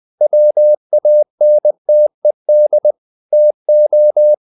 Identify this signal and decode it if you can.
Morse code, heard as a keyed tone, about 20 wpm, 600 Hz. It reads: WANTED TO
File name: the old man and the sea part 2 20wpm